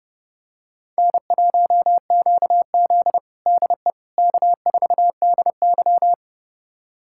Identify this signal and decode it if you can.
Morse code, heard as a keyed tone, about 32 wpm, 700 Hz. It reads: N1QZ DE K4DY
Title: USA Callsign DE Callsign - 4 Characters - 30wpm